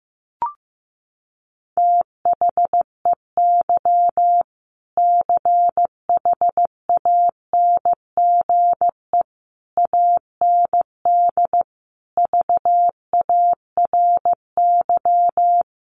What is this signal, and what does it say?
Morse code, heard as a keyed tone, about 15 wpm, 700 Hz. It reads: THEY CHANGE AND VARY
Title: Sentences from Top 1200 Words 15wpm